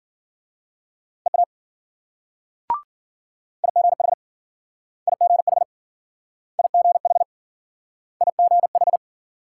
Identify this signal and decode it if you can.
Morse code, heard as a keyed tone, about 51 wpm, 700 Hz. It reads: EA IGH IGH IGH IGH